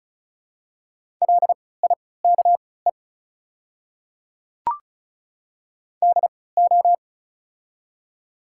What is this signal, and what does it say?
Morse code, heard as a keyed tone, about 36 wpm, spaced as for 20 wpm, 690 Hz. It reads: LIKE DO
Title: 3x Character Spacing - Top 100 Words - 35wpm